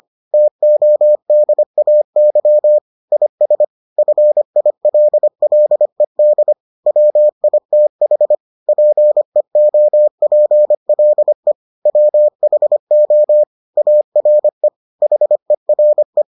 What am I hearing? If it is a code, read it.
Morse code, heard as a keyed tone, about 25 wpm, 600 Hz. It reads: TODAY IS FILLED WITH PEOPLE WHO ARE HERE